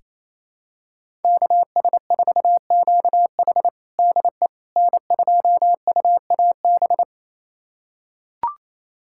Morse code, heard as a keyed tone, about 28 wpm, 700 Hz. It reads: KS4QH DE N2UAB
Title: USA Callsign DE Callsign - 5 Characters - 28wpm